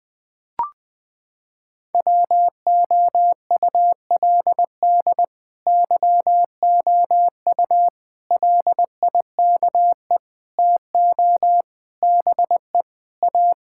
Morse code, heard as a keyed tone, about 21 wpm, 700 Hz. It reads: WOULD YOU LIKE TO BE A
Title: Sentences from Top 1000 Words - Review Part 6 - 20wpm